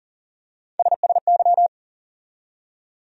Morse code, heard as a keyed tone, about 40 wpm, 700 Hz. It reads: SSY
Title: USA Call Sign Postfixes - 3 Letters 40wpm